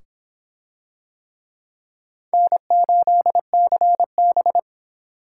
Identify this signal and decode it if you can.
Morse code, heard as a keyed tone, about 24 wpm, 700 Hz. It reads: N8CB